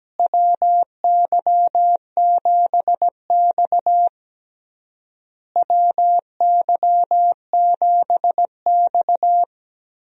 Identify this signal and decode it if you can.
Morse code, heard as a keyed tone, about 17 wpm, 700 Hz. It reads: WY7X WY7X
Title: USA Call Signs - 4 Characters - Repeated 3x - 17wpm